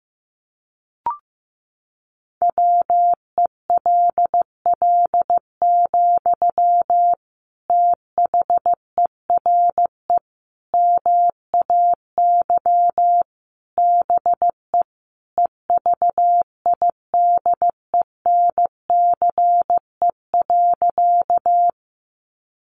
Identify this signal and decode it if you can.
Morse code, heard as a keyed tone, about 15 wpm, 700 Hz. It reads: WELL, THERE MAY BE EVIDENCE.
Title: Sentences from Top 1300 Words 15wpm